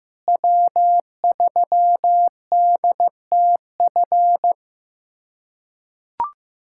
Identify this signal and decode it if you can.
Morse code, heard as a keyed tone, about 15 wpm, 700 Hz. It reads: W3DTF
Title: USA Callsign DE Callsign - 5 Characters - 15wpm